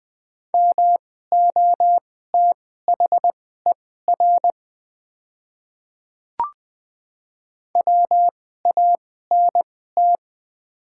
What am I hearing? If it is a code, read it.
Morse code, heard as a keyed tone, about 21 wpm, spaced as for 15 wpm, 700 Hz. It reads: MOTHER WANT
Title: 2x Character Spacing - Top 500 Words - 20wpm